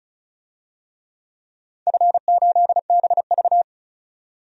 Morse code, heard as a keyed tone, about 36 wpm, 700 Hz. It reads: F8BV
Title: DXCC-100 Call Signs - 4 Characters - 35wpm